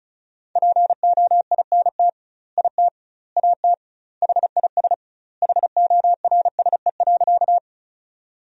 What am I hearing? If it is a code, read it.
Morse code, heard as a keyed tone, about 34 wpm, 700 Hz. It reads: POINT IT AT HIS HORSE.
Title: 2x Word-Spacing - Sentences from Top 300 Words - 35wpm